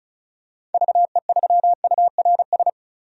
Morse code, heard as a keyed tone, about 36 wpm, 700 Hz. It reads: VE3URS
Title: DXCC-100 Call Signs - 6 Characters - 35wpm